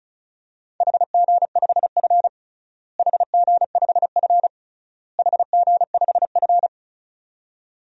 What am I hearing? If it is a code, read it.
Morse code, heard as a keyed tone, about 36 wpm, 700 Hz. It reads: HG5F HG5F HG5F
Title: DXCC-100 Call Signs - 4 Characters - Repeated 3x - 35wpm